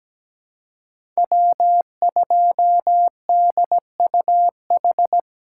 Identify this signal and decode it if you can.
Morse code, heard as a keyed tone, about 17 wpm, 700 Hz. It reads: W2DUH
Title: USA Call Signs - 3x Character Spacing - 17wpm